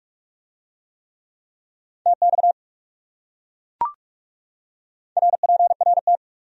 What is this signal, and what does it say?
Morse code, heard as a keyed tone, about 39 wpm, 700 Hz. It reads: TX RPRT